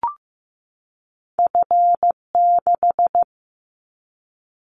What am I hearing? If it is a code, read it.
Morse code, heard as a keyed tone, about 15 wpm, 700 Hz. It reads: F6
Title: Callsign Prefixes for Top 25 DXCC - 15wpm